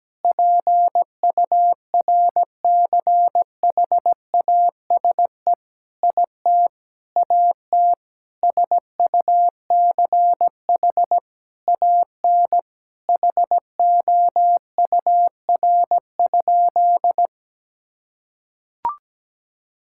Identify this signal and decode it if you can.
Morse code, heard as a keyed tone, about 17 wpm, 700 Hz. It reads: PURCHASE IT AT SUCH AN HOUR?
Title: Sentences from Top 1000 Words - Review Part 5 - 17wpm